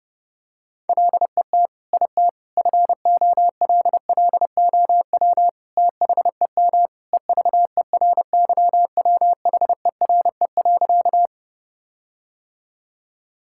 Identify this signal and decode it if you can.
Morse code, heard as a keyed tone, about 30 wpm, 700 Hz. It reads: LET IT FOLLOW THEM EVERYWHERE.
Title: Sentences from Top 1600 Words 30wpm